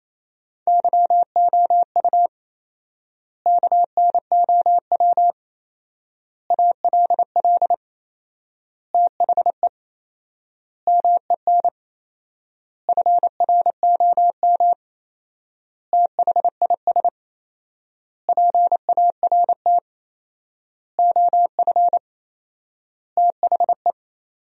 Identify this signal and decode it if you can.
Morse code, heard as a keyed tone, about 27 wpm, 700 Hz. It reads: YOU KNOW ALL THE MEN FROM THIS PART OF THE
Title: 4x Word-Spacing - Sentences from Top 300 Words - 28wpm